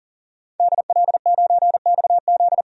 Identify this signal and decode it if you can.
Morse code, heard as a keyed tone, about 40 wpm, 700 Hz. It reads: DL9XZ
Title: DXCC-100 Call Signs - 5 Characters - 40wpm